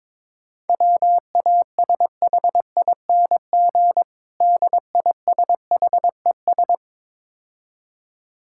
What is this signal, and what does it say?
Morse code, heard as a keyed tone, about 23 wpm, 700 Hz. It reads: WASHING DISHES
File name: Easy Words - Speed-Racing with Hints - 22wpm